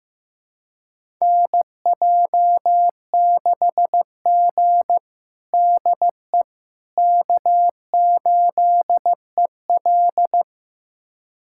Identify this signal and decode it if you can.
Morse code, heard as a keyed tone, about 15 wpm, 700 Hz. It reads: NJ6G DE K8EL
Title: USA Callsign DE Callsign - 4 Characters - 15wpm